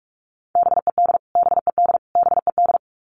Morse code, heard as a keyed tone, about 44 wpm, 700 Hz. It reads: BED BED BED
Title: Top 500 Words Repeated 45wpm